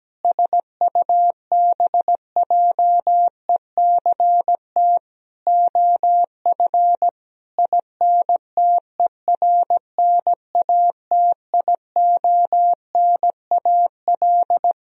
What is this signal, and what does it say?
Morse code, heard as a keyed tone, about 17 wpm, 700 Hz. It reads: SUBJECT OF INTERNATIONAL
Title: Sentences from Top 1000 Words - Review Part 4 - 17wpm